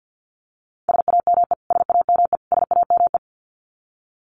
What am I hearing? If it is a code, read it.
Morse code, heard as a keyed tone, about 48 wpm, 700 Hz. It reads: SAME SAME SAME
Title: Top 300 Words Repeated 50wpm